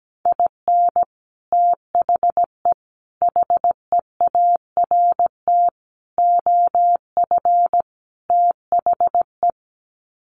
Morse code, heard as a keyed tone, about 17 wpm, 700 Hz. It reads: IN THE HEART OF THE
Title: Sets of 5 Words 17wpm